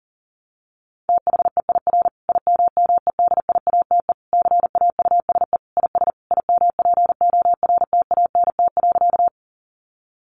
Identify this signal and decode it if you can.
Morse code, heard as a keyed tone, about 37 wpm, 700 Hz. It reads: THEIR IMMEDIATE CAUSE IS IMPORTANT.
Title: Sentences from Top 1200 Words 40wpm